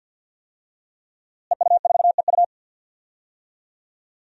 Morse code, heard as a keyed tone, about 46 wpm, 700 Hz. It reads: EU4EU
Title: DXCC-100 Call Signs - 5 Characters - Repeated 2x - 50wpm